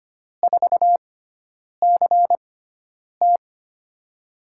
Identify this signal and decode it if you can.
Morse code, heard as a keyed tone, about 25 wpm, spaced as for 9 wpm, 700 Hz. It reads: 4CT
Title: USA Call Signs - 6x Character Spacing - 25wpm